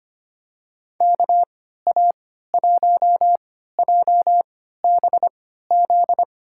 Morse code, heard as a keyed tone, about 25 wpm, spaced as for 14 wpm, 700 Hz. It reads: KA1JBZ
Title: USA Call Signs - 6 Characters - 3x Character Spacing - 25wpm